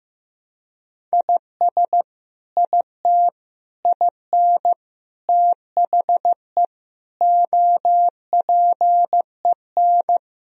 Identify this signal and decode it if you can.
Morse code, heard as a keyed tone, about 15 wpm, 700 Hz. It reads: IS IT IN THE OPEN
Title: Sentences from Top 1000 Words - Review Part 1 - 15wpm